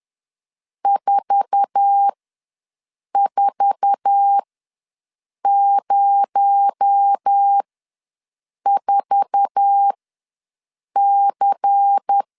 Morse code, heard as a keyed tone, about 10 wpm, spaced as for 6 wpm, 800 Hz. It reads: 4404C
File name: LEZIONI / 8 YJ45 / 8 YJ4 ws 10wpm 4020